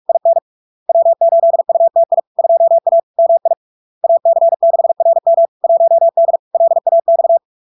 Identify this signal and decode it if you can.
Morse code, heard as a keyed tone, about 46 wpm, 670 Hz. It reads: IN W8UTI 2A MI AC6RM 1D LAX
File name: LARK-FD-2022-Log-in-CW-45wpm-670-Hz